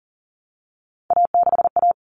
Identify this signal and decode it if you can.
Morse code, heard as a keyed tone, about 40 wpm, 700 Hz. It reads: A6A
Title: Common Cut Numbers - 3-Digit 40wpm